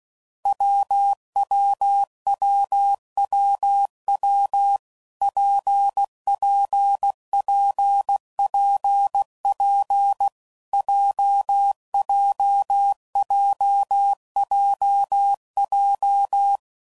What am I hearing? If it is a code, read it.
Morse code, heard as a keyed tone, about 16 wpm, 800 Hz. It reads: WWWWW PPPPP JJJJJ
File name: Combinations / Difficult Combos / WPJ WPJ1